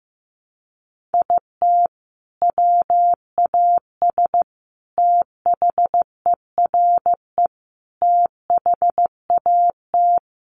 Morse code, heard as a keyed tone, about 15 wpm, 700 Hz. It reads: IT WAS THERE THAT
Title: Sentences from Top 1500 Words 15wpm